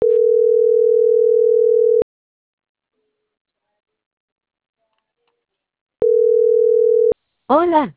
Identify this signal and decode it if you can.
Ringback tone (North America), 440 + 480 Hz.
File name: CENTRAL NMP / out marianacruznmp